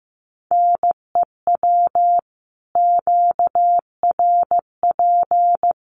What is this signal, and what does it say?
Morse code, heard as a keyed tone, about 15 wpm, 700 Hz. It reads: NEW QRP
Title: Common QSO Phrases 15wpm